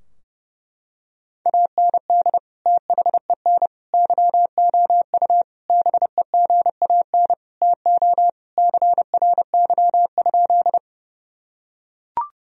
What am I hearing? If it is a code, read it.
Morse code, heard as a keyed tone, about 32 wpm, 700 Hz. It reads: AND THEN YOU BEGAN TO CRY?
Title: Sentences from Top 1000 Words - Review Part 2 - 30wpm